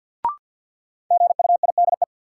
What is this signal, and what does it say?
Morse code, heard as a keyed tone, about 52 wpm, 700 Hz. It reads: GUIDE
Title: Sentences from Top 600 Words - New Word Drill - 50wpm